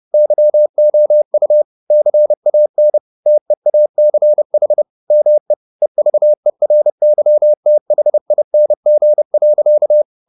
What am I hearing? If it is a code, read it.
Morse code, heard as a keyed tone, about 30 wpm, 600 Hz. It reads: YOU CAN TEACH ME EVERYTHING.